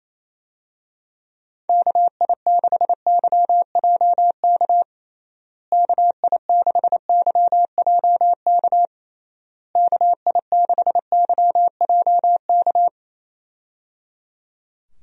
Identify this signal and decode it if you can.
Morse code, heard as a keyed tone, about 27 wpm, 700 Hz. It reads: KI6YJK KI6YJK KI6YJK
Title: USA Call Signs - 6 Characters - Repeated 3x - 28wpm